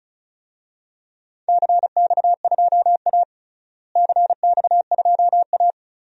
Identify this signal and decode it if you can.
Morse code, heard as a keyed tone, about 36 wpm, 700 Hz. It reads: CX2A CX2A
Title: DXCC-100 Call Signs - 4 Characters - Repeated 3x - 35wpm